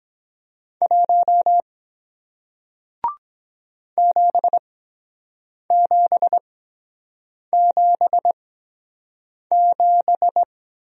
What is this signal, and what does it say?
Morse code, heard as a keyed tone, about 22 wpm, spaced as for 7 wpm, 700 Hz. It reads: 1 7777